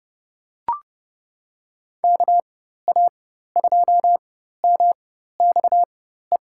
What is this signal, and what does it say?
Morse code, heard as a keyed tone, about 31 wpm, spaced as for 14 wpm, 700 Hz. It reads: KA2MXE
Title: USA Call Signs - 6 Characters - 4x Character Spacing - 30wpm